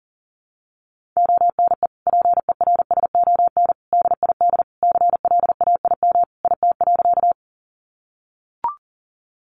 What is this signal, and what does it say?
Morse code, heard as a keyed tone, about 37 wpm, 700 Hz. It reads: ONE PERSON DID CLAIM IT.
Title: Sentences from Top 1500 Words 40wpm